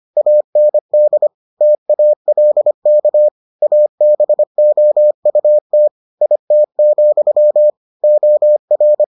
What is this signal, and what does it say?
Morse code, heard as a keyed tone, about 25 wpm, 600 Hz. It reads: AND TALK ABOUT IT, OR